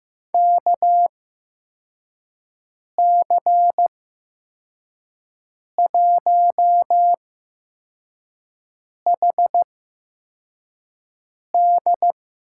Morse code, heard as a keyed tone, about 15 wpm, spaced as for 4 wpm, 700 Hz. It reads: KC1HD
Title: USA Call Signs - 6 Characters - 8x Character Spacing - 15wpm